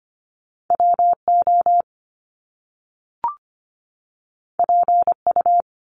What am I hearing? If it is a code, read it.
Morse code, heard as a keyed tone, about 25 wpm, 700 Hz. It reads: WO PU
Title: Top Two Letter Combinations in Words 25wpm